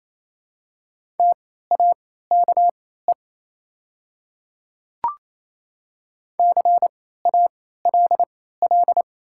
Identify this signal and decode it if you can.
Morse code, heard as a keyed tone, about 27 wpm, spaced as for 16 wpm, 700 Hz. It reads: TAKE CALL
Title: 3x Character Spacing - Top 300 Words - 28wpm